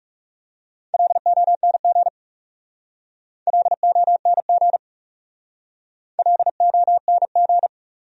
Morse code, heard as a keyed tone, about 36 wpm, 700 Hz. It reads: LONG LONG LONG
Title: Speed-Racing Top 500 Words 30wpm